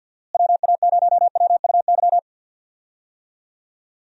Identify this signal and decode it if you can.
Morse code, heard as a keyed tone, about 50 wpm, 700 Hz. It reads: WA0WUY
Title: USA Call Signs - 6 Characters - Repeated 3x - 50wpm